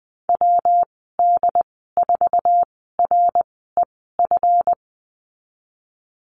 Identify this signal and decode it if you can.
Morse code, heard as a keyed tone, about 20 wpm, spaced as for 15 wpm, 700 Hz. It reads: WD4REF
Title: 167 Lesson 27g - Number 7 - Call Signs 20wpm